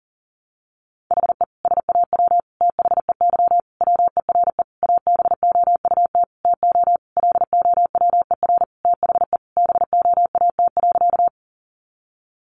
Morse code, heard as a keyed tone, about 40 wpm, 700 Hz. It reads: HE SAW THEY WERE ABOUT TO LOWER THE BOAT.
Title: Sentences from Top 700 Words 40wpm